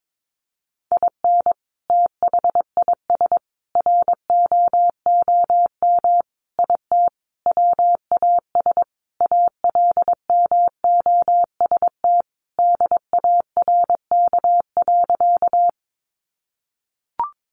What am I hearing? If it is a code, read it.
Morse code, heard as a keyed tone, about 22 wpm, 700 Hz. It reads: IN THIS ROOM IT WAS ALMOST DARK.
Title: Sentences from Top 700 Words 22wpm